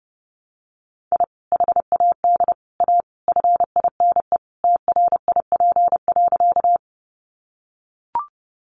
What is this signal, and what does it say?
Morse code, heard as a keyed tone, about 30 wpm, 700 Hz. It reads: I HAD A FINE TRIP.
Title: Sentences from Top 1300 Words 30wpm